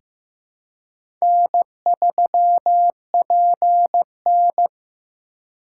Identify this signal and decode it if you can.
Morse code, heard as a keyed tone, about 16 wpm, 700 Hz. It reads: N3PN
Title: USA Call Signs - 4 Characters - 2x Character Spacing - 15wpm